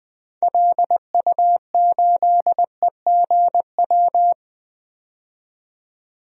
Morse code, heard as a keyed tone, about 21 wpm, 700 Hz. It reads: LU8EGW
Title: DXCC-100 Call Signs - 6 Characters - 20wpm